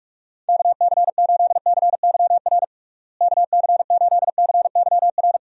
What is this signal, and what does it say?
Morse code, heard as a keyed tone, about 47 wpm, 700 Hz. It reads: KC8CYR KC8CYR
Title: USA Call Signs - 6 Characters - Repeated 3x - 45wpm